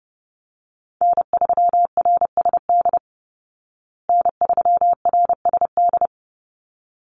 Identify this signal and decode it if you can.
Morse code, heard as a keyed tone, about 29 wpm, 700 Hz. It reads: N3RSD N3RSD
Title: Call Signs Repeated 3x - 30wpm